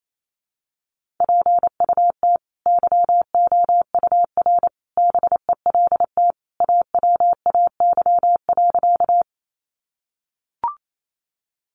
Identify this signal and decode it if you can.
Morse code, heard as a keyed tone, about 28 wpm, 700 Hz. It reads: PUT YOUR BELT AWAY.
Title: Sentences from Top 1400 Words 28wpm